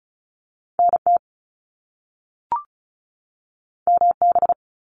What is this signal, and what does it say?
Morse code, heard as a keyed tone, about 36 wpm, 700 Hz. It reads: NT MB
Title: Canadian Provinces and Territories 35wpm